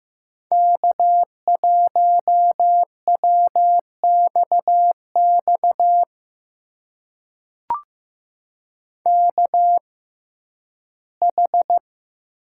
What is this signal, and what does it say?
Morse code, heard as a keyed tone, about 15 wpm, spaced as for 11 wpm, 700 Hz. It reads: K1WXX KH
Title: USA Call Signs - 5 Characters - 6x Character Spacing - 15wpm